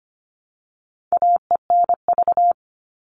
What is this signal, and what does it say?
Morse code, heard as a keyed tone, about 25 wpm, 700 Hz. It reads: AENV
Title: All Cut Numbers - 4-Digit 25wpm